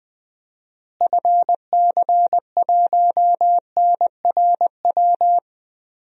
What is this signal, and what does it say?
Morse code, heard as a keyed tone, about 21 wpm, 700 Hz. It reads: FC1NRW
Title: DXCC-100 Call Signs - 6 Characters - Repeated 3x - 20wpm